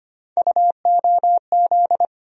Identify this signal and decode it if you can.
Morse code, heard as a keyed tone, about 25 wpm, 700 Hz. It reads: UOZ
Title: USA Call Sign Postfixes - 3 Letters 25wpm